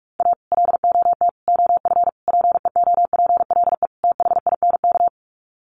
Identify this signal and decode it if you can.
Morse code, heard as a keyed tone, about 44 wpm, 700 Hz. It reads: A LOT OF PEOPLE THINK
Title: Sets of 5 Words Encore 45wpm